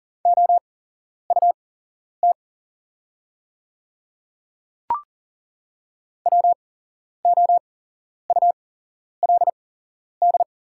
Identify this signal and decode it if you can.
Morse code, heard as a keyed tone, about 40 wpm, spaced as for 11 wpm, 700 Hz. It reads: OUT WOULD
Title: 8x Character Spacing - Top 100 Words - 40wpm